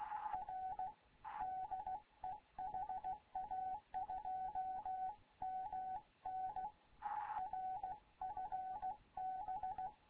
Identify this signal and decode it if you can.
Morse code, heard as a keyed tone, about 16 wpm, 750 Hz, with radio noise. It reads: R DE HA2MN R FB
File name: Morze-HA2MN / hangp / 2 11 GYAKORLAT